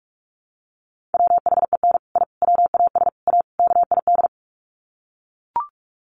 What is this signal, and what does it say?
Morse code, heard as a keyed tone, about 44 wpm, 700 Hz. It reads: WHEN I WAS A KID
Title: Sets of 5 Words 45wpm